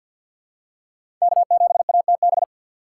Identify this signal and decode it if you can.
Morse code, heard as a keyed tone, about 47 wpm, 700 Hz. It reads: K7ATB
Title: USA Call Signs - 2x Character Spacing - 50wpm